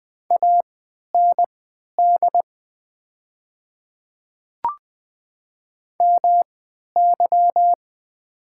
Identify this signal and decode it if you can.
Morse code, heard as a keyed tone, about 20 wpm, spaced as for 11 wpm, 700 Hz. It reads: AND MY
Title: 3x Character Spacing - Top 100 Words - 20wpm